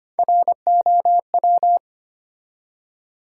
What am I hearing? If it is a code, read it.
Morse code, heard as a keyed tone, about 25 wpm, 700 Hz. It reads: ROW
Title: Three Letter Words - 25wpm